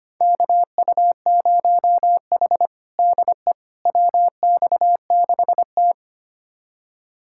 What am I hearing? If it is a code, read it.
Morse code, heard as a keyed tone, about 25 wpm, 700 Hz. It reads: KU0H DE WX6T
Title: USA Callsign DE Callsign - 4 Characters - 25wpm